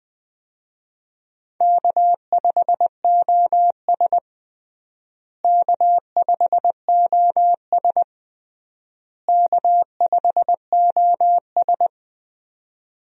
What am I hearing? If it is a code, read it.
Morse code, heard as a keyed tone, about 20 wpm, 700 Hz. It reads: K5OS K5OS K5OS
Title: USA Call Signs - 4 Characters - Repeated 3x - 20wpm